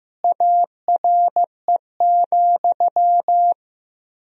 Morse code, heard as a keyed tone, about 15 wpm, 700 Hz. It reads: ARE,
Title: 3x Word-Spacing - Sentences from Top 100 Words - 15wpm